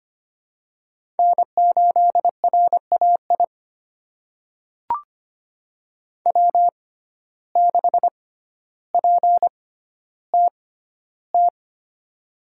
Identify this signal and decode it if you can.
Morse code, heard as a keyed tone, about 25 wpm, spaced as for 13 wpm, 700 Hz. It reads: N8RAI W6PTT
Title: USA Call Signs - 5 Characters - 6x Character Spacing - 25wpm